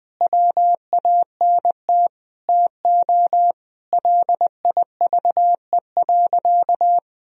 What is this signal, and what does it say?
Morse code, heard as a keyed tone, about 20 wpm, 700 Hz. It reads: WANT TO LIVE.
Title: Sentences from Top 1000 Words - Review Part 3 - 20wpm